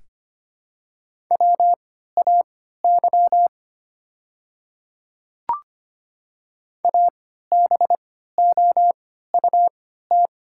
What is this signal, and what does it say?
Morse code, heard as a keyed tone, about 25 wpm, spaced as for 14 wpm, 700 Hz. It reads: WAY ABOUT
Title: 3x Character Spacing - Top 200 Words - 25wpm